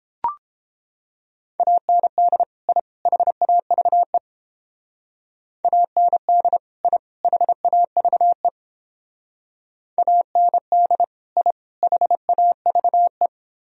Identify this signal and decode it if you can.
Morse code, heard as a keyed tone, about 29 wpm, 700 Hz. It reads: AND I HAVE AND I HAVE AND I HAVE